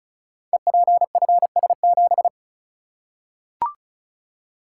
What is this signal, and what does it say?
Morse code, heard as a keyed tone, about 36 wpm, 700 Hz. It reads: EPFS7
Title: Code Groups - 5 Letter:Numbers - 35wpm